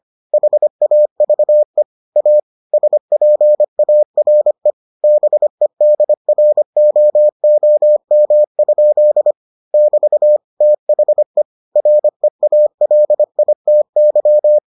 Morse code, heard as a keyed tone, about 25 wpm, 600 Hz. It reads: HAVE A SPARE BEDROOM? <BT> THE REALITY